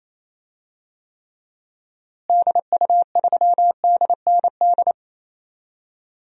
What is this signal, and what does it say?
Morse code, heard as a keyed tone, about 28 wpm, 700 Hz. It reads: DU3DND
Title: DXCC-100 Call Signs - 6 Characters - Repeated 2x - 28wpm